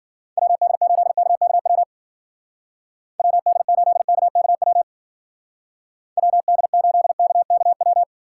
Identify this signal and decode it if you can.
Morse code, heard as a keyed tone, about 49 wpm, 700 Hz. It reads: WD8KKW WD8KKW WD8KKW
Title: Speed Racing - USA 2x3 Call Signs 40wpm